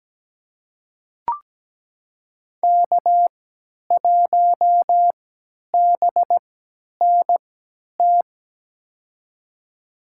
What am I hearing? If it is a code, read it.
Morse code, heard as a keyed tone, about 17 wpm, spaced as for 10 wpm, 700 Hz. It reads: K1BNT
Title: USA Call Signs - 5 Characters - 3x Character Spacing - 17wpm